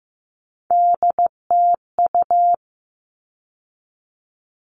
Morse code, heard as a keyed tone, about 15 wpm, 700 Hz. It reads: DTU
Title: All Cut Numbers - 3-Digit 15wpm